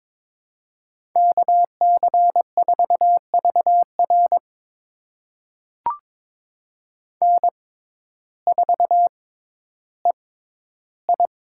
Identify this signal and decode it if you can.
Morse code, heard as a keyed tone, about 22 wpm, spaced as for 12 wpm, 700 Hz. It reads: KC4VR N4EI
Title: USA Call Signs - 5 Characters - 6x Character Spacing - 22wpm